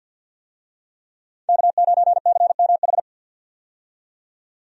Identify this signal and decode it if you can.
Morse code, heard as a keyed tone, about 48 wpm, 700 Hz. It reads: K9CMH
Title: USA Call Signs - 5 Characters - 50wpm